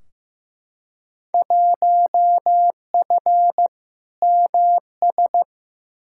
Morse code, heard as a keyed tone, about 15 wpm, 700 Hz. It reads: 1F MS
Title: Field Day Exchanges Encore - 15wpm